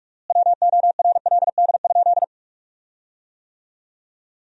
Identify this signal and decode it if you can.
Morse code, heard as a keyed tone, about 46 wpm, 700 Hz. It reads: WORLD?
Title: 4x Word-Spacing - Sentences from Top 200 Words - 45wpm